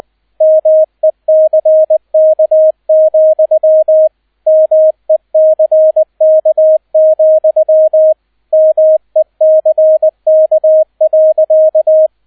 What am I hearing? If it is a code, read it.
Morse code, heard as a keyed tone, about 20 wpm, 630 Hz, with radio noise. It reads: MECK, MECK, MECK.